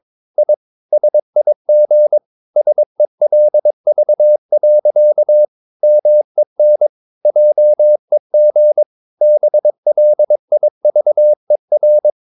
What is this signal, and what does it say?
Morse code, heard as a keyed tone, about 22 wpm, 600 Hz. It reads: I SIG SELV. MEN JEG BLIVER
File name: Ferie QRS 22wpm